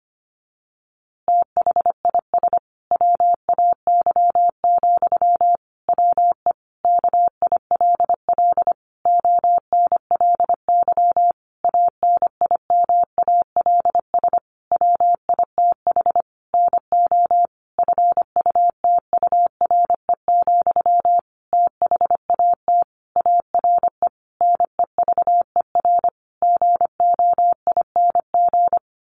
Morse code, heard as a keyed tone, about 25 wpm, 700 Hz. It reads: THIS WAY, WE KILL ONLY ANIMALS WITH NO FUTURE, THAT ARE NEVER GOING
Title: A Sound Of Thunder 25wpm